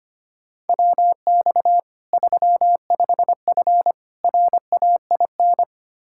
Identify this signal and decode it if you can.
Morse code, heard as a keyed tone, about 25 wpm, 700 Hz. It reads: WX 35F RAIN
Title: K1USN SST QSO Practice 25wpm